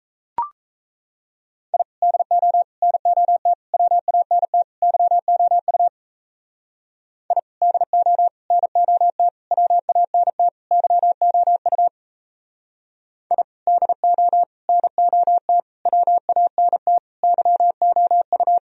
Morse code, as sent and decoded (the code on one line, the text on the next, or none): .. / -.. --- / -. --- - / .-- .- -. - / -.-- --- ..- / .. / -.. --- / -. --- - / .-- .- -. - / -.-- --- ..- / .. / -.. --- / -. --- - / .-- .- -. - / -.-- --- ..-
I DO NOT WANT YOU I DO NOT WANT YOU I DO NOT WANT YOU